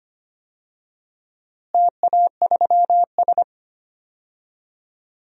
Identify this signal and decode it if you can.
Morse code, heard as a keyed tone, about 25 wpm, 700 Hz. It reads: TA3S
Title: DXCC-100 Call Signs - 4 Characters - Repeated 3x - 25wpm